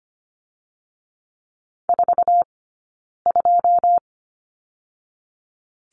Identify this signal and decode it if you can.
Morse code, heard as a keyed tone, about 25 wpm, 700 Hz. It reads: 4 2
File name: Lesson 17d - Letter u - Two Characters 25wpm